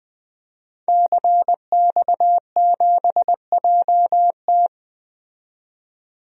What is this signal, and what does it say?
Morse code, heard as a keyed tone, about 20 wpm, 700 Hz. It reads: CX7JT
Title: DXCC-100 Call Signs - 5 Characters - 20wpm